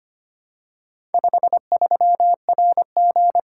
Morse code, heard as a keyed tone, about 25 wpm, 700 Hz. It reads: 53RG
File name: practice / 25wpm / Four Letter-Number Combinations - 25wpm